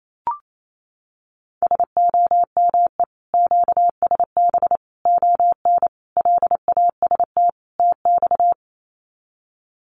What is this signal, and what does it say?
Morse code, heard as a keyed tone, about 28 wpm, 700 Hz. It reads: SOME QSB ON LAST TX
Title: Common QSO Phrases 28wpm